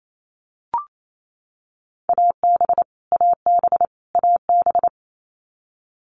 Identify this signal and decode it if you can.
Morse code, heard as a keyed tone, about 28 wpm, 700 Hz. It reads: AB AB AB
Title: Canadian Provinces and Territories - Repeated 3x - 28wpm